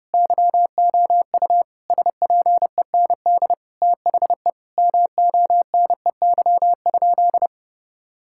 Morse code, heard as a keyed tone, about 32 wpm, 700 Hz. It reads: YOU SPEND THE MONEY?
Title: Sentences from Top 1000 Words - Review Part 5 - 30wpm